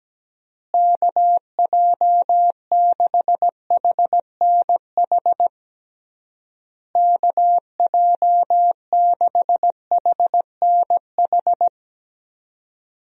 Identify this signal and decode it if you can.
Morse code, heard as a keyed tone, about 17 wpm, 700 Hz. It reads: KJ6HNH KJ6HNH
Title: USA Call Signs - 6 Characters - Repeated 3x - 17wpm